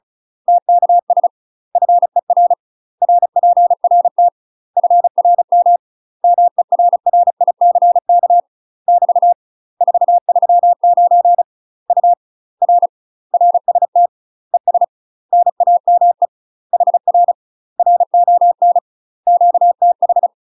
Morse code, heard as a keyed tone, about 34 wpm, 700 Hz. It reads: TKS FER RPRT FRM MERRICK <BT> 439 U R RST ES NAME HR RON QTH
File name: UK MASTER QSOs TRAINING / 35WPM / UK MASTER QSOs 0038